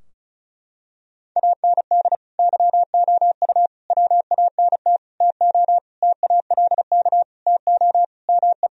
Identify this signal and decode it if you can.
Morse code, heard as a keyed tone, about 36 wpm, 700 Hz. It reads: AND YOU WANT TO TALK TO ME
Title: Sentences from Top 1000 Words - Review Part 4 - 35wpm